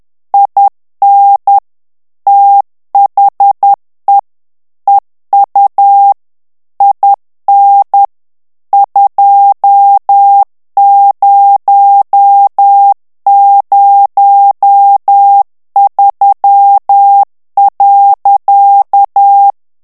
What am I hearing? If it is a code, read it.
Morse code, heard as a keyed tone, about 10 wpm, 800 Hz. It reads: IN THE EU IN 2003.
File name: morse mp3 / ANGLAIS texte clair 2 vitesse 10 mots minute